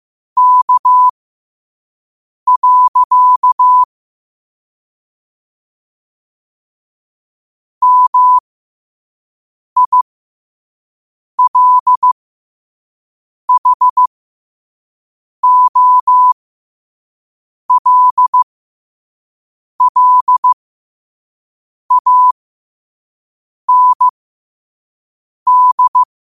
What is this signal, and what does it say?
Morse code, heard as a keyed tone, about 15 wpm, spaced as for 5 wpm, 1000 Hz. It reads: K. MILHOLLAND
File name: qotdfiles / 5wpm